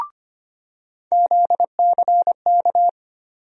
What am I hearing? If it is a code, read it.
Morse code, heard as a keyed tone, about 25 wpm, 700 Hz. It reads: ZCK
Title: USA Call Sign Postfixes - 3 Letters 25wpm